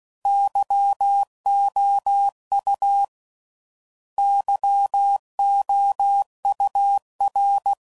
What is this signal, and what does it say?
Morse code, heard as a keyed tone, about 16 wpm, 800 Hz. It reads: YOU YOUR